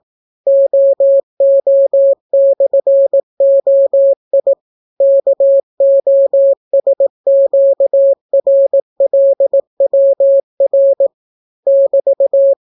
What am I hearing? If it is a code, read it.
Morse code, heard as a keyed tone, about 18 wpm, 550 Hz. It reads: OO/OI KOSQRLWR <BT>